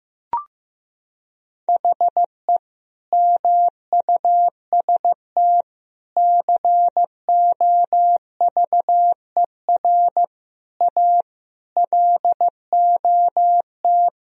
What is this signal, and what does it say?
Morse code, heard as a keyed tone, about 15 wpm, 700 Hz. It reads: HE MUST COVER A LOT
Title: Sentences from Top 1000 Words - Review Part 3 - 15wpm